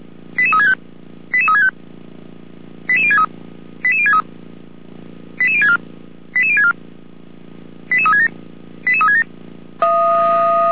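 Five-tone selective call (ZVEI), heard as quick runs of five tones, tones of about 70 ms. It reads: 88365 88365 88063 88063 88064 88064 88367 88367
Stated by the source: Five-tone ZVEI selective call received on SDR, from the audio server of Feuerwehr Orlamünde (fire brigade).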